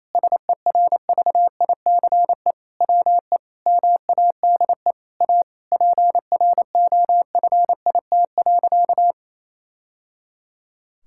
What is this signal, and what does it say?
Morse code, heard as a keyed tone, about 28 wpm, 700 Hz. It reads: SERVICE WE MADE A PROFIT.
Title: Sentences from Top 1000 Words - Review Part 5 - 28wpm